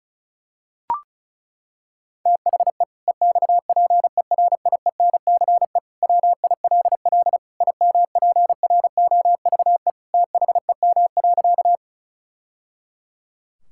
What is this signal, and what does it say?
Morse code, heard as a keyed tone, about 36 wpm, 700 Hz. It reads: THE EXPERIENCE WILL IMPROVE THEM.
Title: Sentences from Top 1000 Words - Review Part 5 - 35wpm